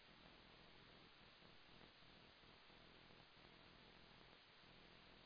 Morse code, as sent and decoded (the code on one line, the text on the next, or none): none